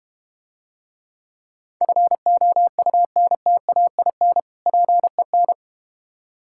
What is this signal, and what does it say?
Morse code, heard as a keyed tone, about 32 wpm, 690 Hz. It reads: FOUNTAIN PEN